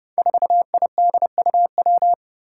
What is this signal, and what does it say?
Morse code, heard as a keyed tone, about 32 wpm, 700 Hz. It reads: 4IDUW
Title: Five Challenging Letter-Number Combinations 30wpm